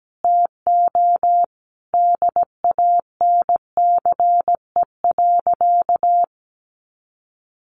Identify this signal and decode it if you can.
Morse code, heard as a keyed tone, about 17 wpm, 700 Hz. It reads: TO DANCE.
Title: Sentences from Top 900 Words 17wpm